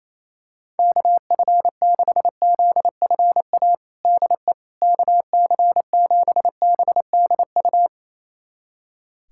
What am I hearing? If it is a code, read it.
Morse code, heard as a keyed tone, about 28 wpm, 700 Hz. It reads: KF6ZFA DE KC7BDU
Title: USA Callsign DE Callsign - 6 Characters - 28wpm